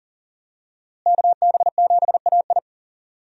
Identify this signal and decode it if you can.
Morse code, heard as a keyed tone, about 40 wpm, 700 Hz. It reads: KB7AI
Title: USA Call Signs - 5 Characters - Repeated 3x - 40wpm